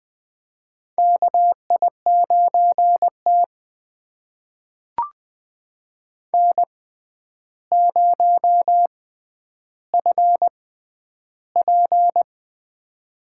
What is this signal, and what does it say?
Morse code, heard as a keyed tone, about 20 wpm, spaced as for 10 wpm, 700 Hz. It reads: KI9T N0FP
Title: USA Call Signs - 4 Characters - 6x Character Spacing - 20wpm